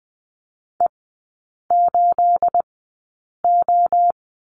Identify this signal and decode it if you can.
Morse code, heard as a keyed tone, about 20 wpm, spaced as for 8 wpm, 690 Hz. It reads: E8O
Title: 212 Lesson 34e - Number 8 - Three Characters 20wpm